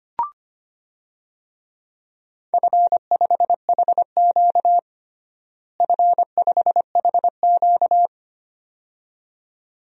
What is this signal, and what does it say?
Morse code, heard as a keyed tone, about 25 wpm, 700 Hz. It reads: F5HQ F5HQ
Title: DXCC-100 Call Signs - 4 Characters - Repeated 2x - 25wpm